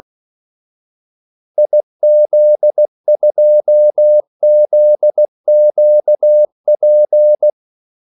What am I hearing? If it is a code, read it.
Morse code, heard as a keyed tone, about 16 wpm, 600 Hz. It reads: IZ2ZQP
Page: Kaldesignaler 16 wpm